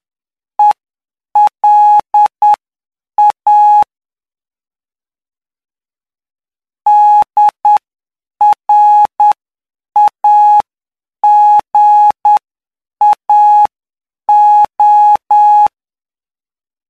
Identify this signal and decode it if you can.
Morse code, heard as a keyed tone, about 9 wpm, spaced as for 7 wpm, 840 Hz. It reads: ELA DRAGAO
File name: cw / py2lcd / 07-PY2LCD-Li��o 16